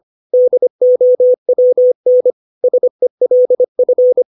Morse code, heard as a keyed tone, about 25 wpm, 500 Hz. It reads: DOWN SELF
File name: 632 words / 632 Words 25WPM